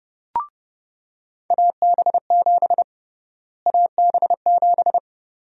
Morse code, heard as a keyed tone, about 30 wpm, 700 Hz. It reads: AB7 AB7
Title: USA Call Sign Prefixes - Repeated 2x - 30wpm